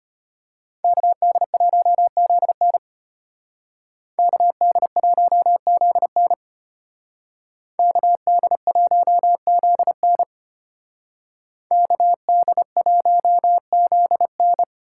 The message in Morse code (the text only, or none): KD1ZN KD1ZN KD1ZN KD1ZN